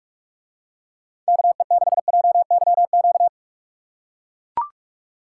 Morse code, heard as a keyed tone, about 44 wpm, 700 Hz. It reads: KE6JYQ